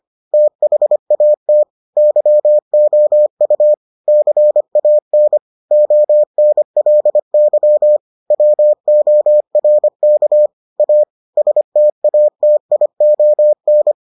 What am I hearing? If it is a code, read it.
Morse code, heard as a keyed tone, about 25 wpm, 600 Hz. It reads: THAT YOU CAN ONLY WORK A STATION